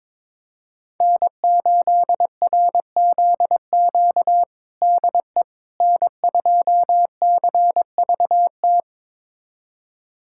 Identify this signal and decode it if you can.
Morse code, heard as a keyed tone, about 22 wpm, 700 Hz. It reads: N8RZQ DE N2CVT
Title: USA Callsign DE Callsign - 5 Characters - 22wpm